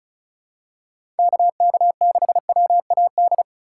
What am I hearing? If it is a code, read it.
Morse code, heard as a keyed tone, about 36 wpm, 700 Hz. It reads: KK6WAD
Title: USA Call Signs - 6 Characters - 4x Character Spacing - 35wpm